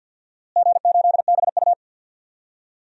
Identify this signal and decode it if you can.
Morse code, heard as a keyed tone, about 50 wpm, 700 Hz. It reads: G8BU
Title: DXCC-100 Call Signs - 4 Characters - Repeated 3x - 50wpm